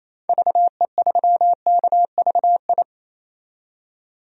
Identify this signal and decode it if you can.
Morse code, heard as a keyed tone, about 28 wpm, 700 Hz. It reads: VE3KVI
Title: DXCC-100 Call Signs - 6 Characters - 28wpm